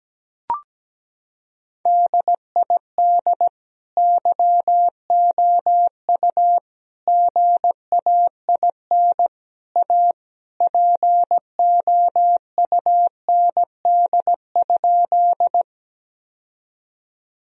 Morse code, heard as a keyed tone, about 17 wpm, 700 Hz. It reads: DID YOU GAIN A POUND?
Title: Sentences from Top 1000 Words - Review Part 2 - 17wpm